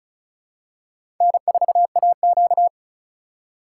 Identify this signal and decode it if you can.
Morse code, heard as a keyed tone, about 34 wpm, 700 Hz. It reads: N4AQ
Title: USA Call Signs - 4 Characters - 4x Character Spacing - 35wpm